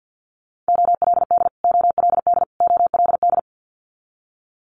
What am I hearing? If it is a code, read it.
Morse code, heard as a keyed tone, about 50 wpm, 690 Hz. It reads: OLD OLD OLD
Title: Multi-letter Phonemes Repeated 3x 50wpm